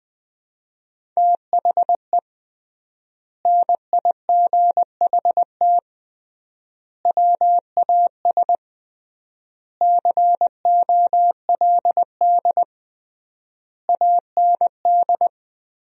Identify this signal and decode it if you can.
Morse code, heard as a keyed tone, about 21 wpm, 700 Hz. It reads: THE NIGHT WAS COLD AND
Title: 3x Word-Spacing - Sentences from Top 400 Words - 20wpm